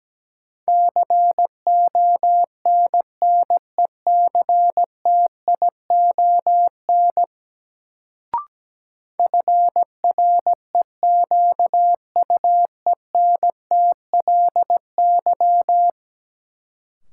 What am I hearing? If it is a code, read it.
Morse code, heard as a keyed tone, about 17 wpm, 700 Hz. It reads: CONNECTION FREQUENTLY
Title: Sentences from Top 1300 Words - New Word Drill Alt - 17wpm